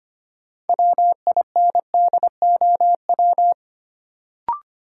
Morse code, heard as a keyed tone, about 25 wpm, 700 Hz. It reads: WINDOW
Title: Sentences from Top 1600 Words - New Word Drill - 25wpm